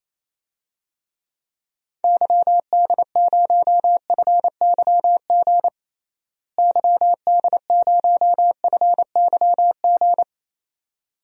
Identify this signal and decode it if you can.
Morse code, heard as a keyed tone, about 28 wpm, 700 Hz. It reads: YD0FYG YD0FYG
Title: DXCC-100 Call Signs - 6 Characters - Repeated 2x - 28wpm